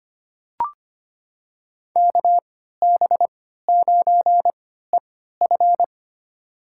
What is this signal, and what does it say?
Morse code, heard as a keyed tone, about 25 wpm, spaced as for 14 wpm, 700 Hz. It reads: KB9EF
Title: USA Call Signs - 3x Character Spacing - 25wpm